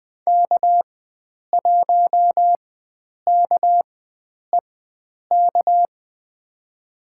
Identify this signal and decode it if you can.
Morse code, heard as a keyed tone, about 21 wpm, spaced as for 9 wpm, 700 Hz. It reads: K1KEK
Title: USA Call Signs - 5 Characters - 4x Character Spacing - 20wpm